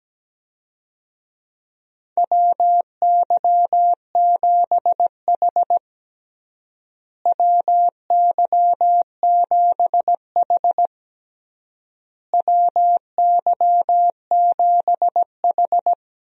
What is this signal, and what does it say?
Morse code, heard as a keyed tone, about 17 wpm, 700 Hz. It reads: WY7H WY7H WY7H
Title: USA Call Signs - 4 Characters - Repeated 3x - 17wpm